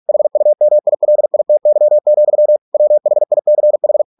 Morse code, heard as a keyed tone, about 46 wpm, 600 Hz. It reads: HUMILITY, WHICH